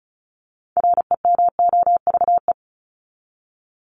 Morse code, heard as a keyed tone, about 34 wpm, 700 Hz. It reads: REMOVE
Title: ED Suffix - Top 100 Words with Base Words 35wpm